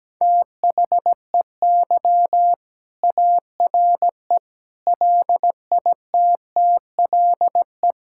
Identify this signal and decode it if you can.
Morse code, heard as a keyed tone, about 17 wpm, 700 Hz. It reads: THEY ARE LITTLE
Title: Sentences from Top 1600 Words 17wpm